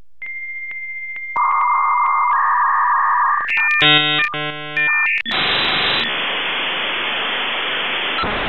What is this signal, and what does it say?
Dial-up modem handshake.